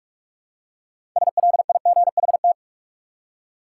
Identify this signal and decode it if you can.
Morse code, heard as a keyed tone, about 43 wpm, 700 Hz. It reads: SLIGHT
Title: Six Letter Words - 45wpm